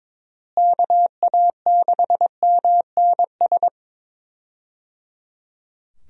Morse code, heard as a keyed tone, about 22 wpm, 700 Hz. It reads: KA6MNS
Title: USA Call Signs - 6 Characters - Repeated 3x - 22wpm